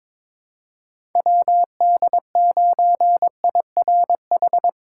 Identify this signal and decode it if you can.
Morse code, heard as a keyed tone, about 22 wpm, 700 Hz. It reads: WD9IRH
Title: USA Call Signs - 2x Character Spacing - 22wpm